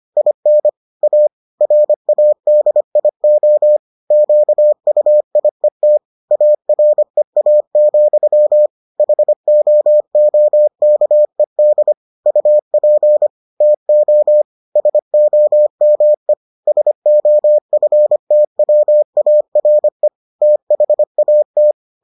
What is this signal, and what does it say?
Morse code, heard as a keyed tone, about 25 wpm, 600 Hz. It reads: IN A RADIO QUIET AREA, HOOKED UP TO SOME SOFTWARE THAT